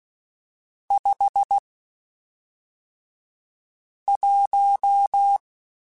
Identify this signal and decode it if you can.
Morse code, heard as a keyed tone, about 16 wpm, spaced as for 3 wpm, 800 Hz. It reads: 51
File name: Spaced Characters / Spaced Characters-mp3s / Weather 4 21 sp @5